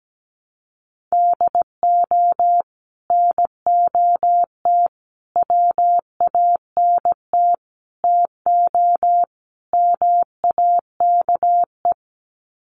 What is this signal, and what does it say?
Morse code, heard as a keyed tone, about 17 wpm, 700 Hz. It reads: DO NOT WANT TO MAKE
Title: Sets of 5 Words Encore 17wpm